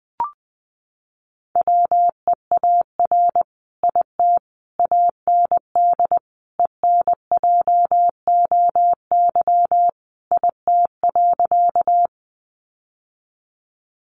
Morse code, heard as a keyed tone, about 20 wpm, 700 Hz. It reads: WEAR IT AND ENJOY IT.
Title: Sentences from Top 1500 Words 20wpm